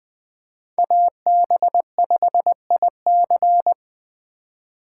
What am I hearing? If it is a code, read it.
Morse code, heard as a keyed tone, about 24 wpm, 700 Hz. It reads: AB5IC
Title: Speed Racing - USA 2x2 Call Signs 20wpm